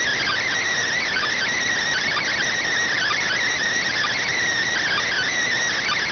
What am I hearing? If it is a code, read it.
Emergency Alert System (SAME) tones (header bursts), heard as harsh buzzing bursts. It is heard from 0.9 to 5.4 s.